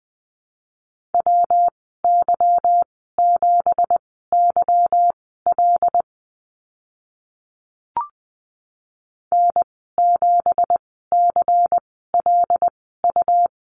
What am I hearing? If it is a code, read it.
Morse code, heard as a keyed tone, about 20 wpm, spaced as for 15 wpm, 700 Hz. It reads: WY7YL N7CLU
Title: 173 Lesson 28g - Number 9 - Call Signs 20wpm